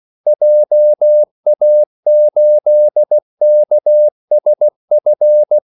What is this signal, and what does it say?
Morse code, heard as a keyed tone, about 16 wpm, 600 Hz. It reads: JA8KSF
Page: Kaldesignaler 16 wpm